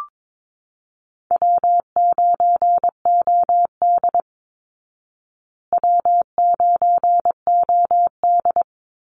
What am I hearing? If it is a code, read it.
Morse code, heard as a keyed tone, about 22 wpm, 700 Hz. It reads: W9OD W9OD
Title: Easy Call Signs Repeated 3x - 22wpm